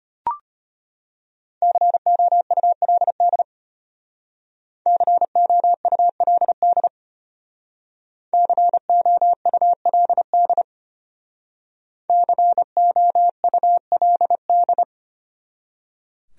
Morse code, heard as a keyed tone, about 32 wpm, 700 Hz. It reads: COULD COULD COULD COULD